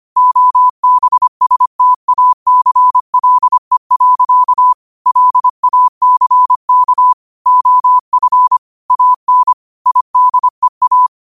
Morse code, heard as a keyed tone, about 25 wpm, 1000 Hz. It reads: OBSTACLE. LACK OF AN IDEA
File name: qotdfiles / 25wpm